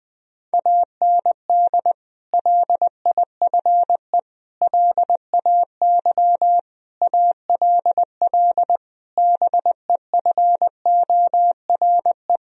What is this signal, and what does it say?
Morse code, heard as a keyed tone, about 20 wpm, 700 Hz. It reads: AND LIFE LAY ALL BEFORE
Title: Sentences from Top 1000 Words - Review Part 2 - 20wpm